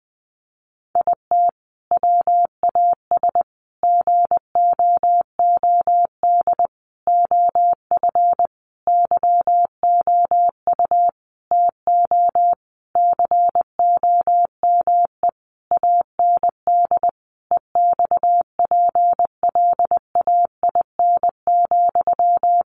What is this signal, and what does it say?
Morse code, heard as a keyed tone, about 20 wpm, 700 Hz. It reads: IT WAS GOOD OF YOU TO COME AND EXPLAIN,
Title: Sentences from Top 1100 Words 20wpm